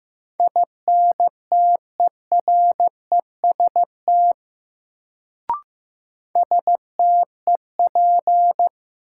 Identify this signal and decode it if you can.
Morse code, heard as a keyed tone, about 15 wpm, 700 Hz. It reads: INTEREST STEP
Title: Sets of 5 Words Encore - New Word Drill - 15wpm